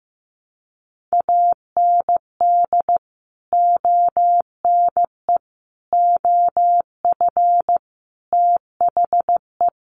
Morse code, heard as a keyed tone, about 15 wpm, 700 Hz. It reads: AND ONE OF THE
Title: Sets of 5 Words Encore 15wpm